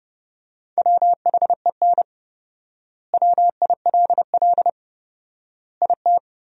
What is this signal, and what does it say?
Morse code, heard as a keyed tone, about 32 wpm, 700 Hz. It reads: WHEN WILL IT
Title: 4x Word-Spacing - Sentences from Top 200 Words - 30wpm